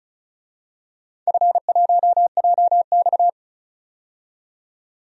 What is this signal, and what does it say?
Morse code, heard as a keyed tone, about 36 wpm, 700 Hz. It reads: F1JX
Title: DXCC-100 Call Signs - 4 Characters - 35wpm